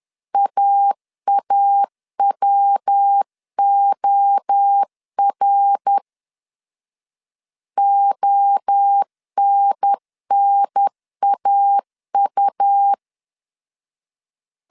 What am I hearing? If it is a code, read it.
Morse code, heard as a keyed tone, about 10 wpm, 800 Hz. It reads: AAWOR ONNAU
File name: CW / DET / 1 3 1500C S A 10wpm 3447